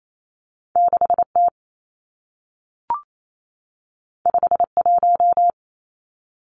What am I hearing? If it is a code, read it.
Morse code, heard as a keyed tone, about 28 wpm, 700 Hz. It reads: 6 T 5 1
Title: Two Challenging Letter-Number Combinations 28wpm